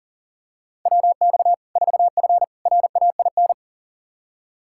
Morse code, heard as a keyed tone, about 40 wpm, 700 Hz. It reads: WX 4F RAIN
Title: Practice QSO - Exchange 2 - Info - 40wpm